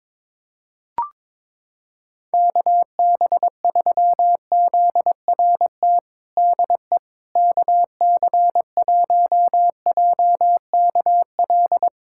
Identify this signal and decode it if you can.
Morse code, heard as a keyed tone, about 22 wpm, 700 Hz. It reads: KB3ZRT DE KC1JKL
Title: USA Callsign DE Callsign 22wpm